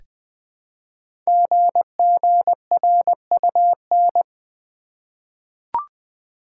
Morse code, heard as a keyed tone, about 20 wpm, 700 Hz. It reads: GGRUN
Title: Code Groups - 5 Letters - 20wpm